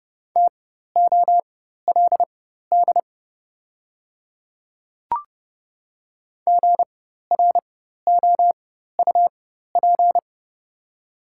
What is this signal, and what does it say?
Morse code, heard as a keyed tone, about 30 wpm, spaced as for 14 wpm, 690 Hz. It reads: TOLD GROUP
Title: 4x Character Spacing - Top 500 Words - 30wpm